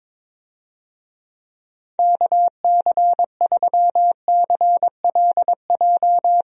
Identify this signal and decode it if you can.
Morse code, heard as a keyed tone, about 22 wpm, 700 Hz. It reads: KC3CLJ
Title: USA Call Signs - 6 Characters - Repeated 3x - 22wpm